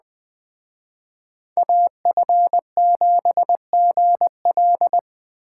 Morse code, heard as a keyed tone, about 20 wpm, 700 Hz. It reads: AF7GL